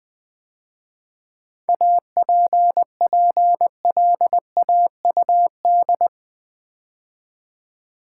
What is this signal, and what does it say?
Morse code, heard as a keyed tone, about 20 wpm, 700 Hz. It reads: APPLAUD
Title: 7 Letter Words - First 3 Letters Given - 20wpm